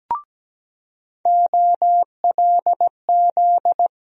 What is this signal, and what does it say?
Morse code, heard as a keyed tone, about 17 wpm, 700 Hz. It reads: OLZ
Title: USA Call Sign Postfixes - 3 Letters 17wpm